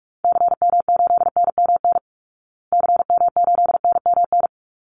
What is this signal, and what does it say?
Morse code, heard as a keyed tone, about 44 wpm, 700 Hz. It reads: CM8NMN CM8NMN
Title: Call Signs Repeated 3x - 45wpm